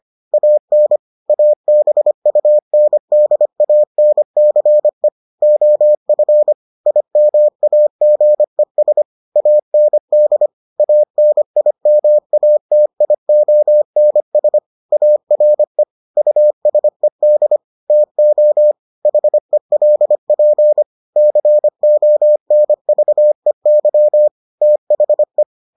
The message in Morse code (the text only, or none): AN ABUNDANCE OF IMAGES AND ANIMATIONS ARE USED TO HELP CONVEY THE